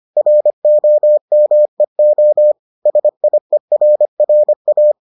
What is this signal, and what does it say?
Morse code, heard as a keyed tone, about 25 wpm, 600 Hz. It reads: ROMEO SIERRA